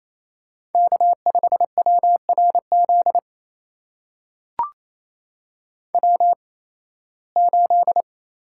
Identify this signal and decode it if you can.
Morse code, heard as a keyed tone, about 27 wpm, spaced as for 18 wpm, 700 Hz. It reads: K5WRZ W8
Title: USA Call Signs - 5 Characters - 8x Character Spacing - 28wpm